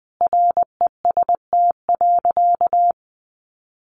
Morse code, heard as a keyed tone, about 20 wpm, 700 Hz. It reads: REST.
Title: Sentences from Top 1300 Words 20wpm